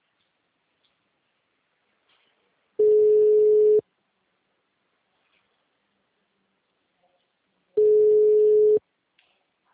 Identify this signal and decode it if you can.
Ringback tone (Europe, 425 Hz).
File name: out- -2002- - - .28763